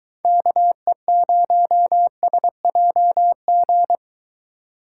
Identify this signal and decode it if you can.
Morse code, heard as a keyed tone, about 23 wpm, 700 Hz. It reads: KE0SJG